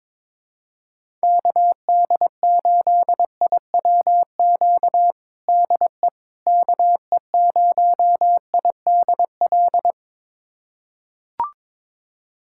Morse code, heard as a keyed tone, about 22 wpm, 700 Hz. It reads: KD8IWQ DE KE0IDL
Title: USA Callsign DE Callsign - 6 Characters - 22wpm